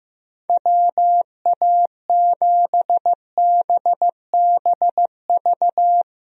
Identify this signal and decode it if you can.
Morse code, heard as a keyed tone, about 16 wpm, 700 Hz. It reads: WA7BBV
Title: USA Call Signs - 6 Characters - Repeated 3x - 15wpm